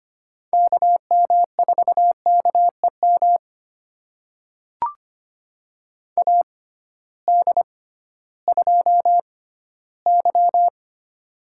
Morse code, heard as a keyed tone, about 25 wpm, spaced as for 15 wpm, 700 Hz. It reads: KM4KEM AD2Y
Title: USA Call Signs - 6x Character Spacing - 25wpm